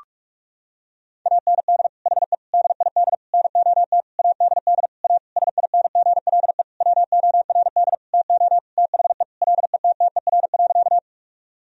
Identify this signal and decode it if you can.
Morse code, heard as a keyed tone, about 42 wpm, 700 Hz. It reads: AND HE DID NOT ADD A SINGLE WORD TO THE LETTER.
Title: Sentences from Top 1000 Words - Review Part 1 - 45wpm